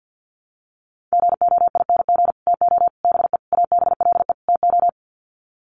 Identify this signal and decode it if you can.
Morse code, heard as a keyed tone, about 50 wpm, 700 Hz. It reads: GOING TO BE ABLE TO
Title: Sets of 5 Words 50wpm